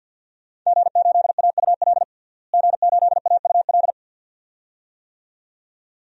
Morse code, heard as a keyed tone, about 50 wpm, 700 Hz. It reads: G8AUL G8AUL
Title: DXCC-100 Call Signs - 5 Characters - Repeated 2x - 50wpm